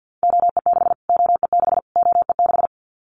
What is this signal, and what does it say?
Morse code, heard as a keyed tone, about 50 wpm, 700 Hz. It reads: OE6 OE6 OE6
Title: Callsign Prefixes for Top 25 DXCC - Repeated 3x - 50wpm